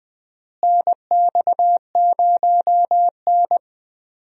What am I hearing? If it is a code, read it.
Morse code, heard as a keyed tone, about 20 wpm, 700 Hz. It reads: NX0N
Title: USA Call Signs - 4 Characters - Repeated 2x - 20wpm